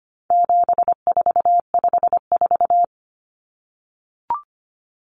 Morse code, heard as a keyed tone, about 25 wpm, 700 Hz. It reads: 7454
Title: Numbers - 4-Digit 25wpm